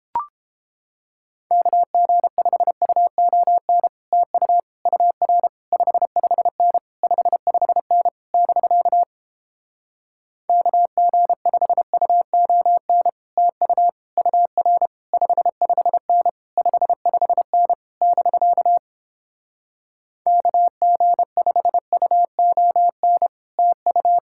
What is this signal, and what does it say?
Morse code, heard as a keyed tone, about 29 wpm, 700 Hz. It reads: KG5UON TU UR 55N 55N <BK> KG5UON TU UR 55N 55N <BK> KG5UON TU